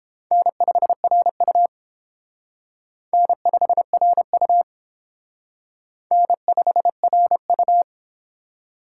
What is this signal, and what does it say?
Morse code, heard as a keyed tone, about 28 wpm, 700 Hz. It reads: N5RU N5RU N5RU